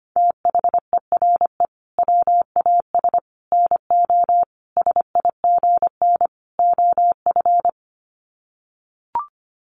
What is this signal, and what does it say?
Morse code, heard as a keyed tone, about 25 wpm, 700 Hz. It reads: THERE WAS NO SIGN OF
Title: Sets of 5 Words Encore2 25wpm